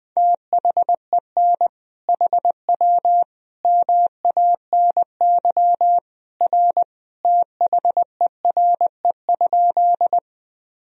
Morse code, heard as a keyed tone, about 21 wpm, 700 Hz. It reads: THEN HW MANY R THERE?
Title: Sentences from Top 100 Words and Abbreviations 20wpm